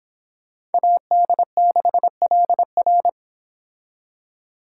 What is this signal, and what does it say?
Morse code, heard as a keyed tone, about 28 wpm, 700 Hz. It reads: AD6LR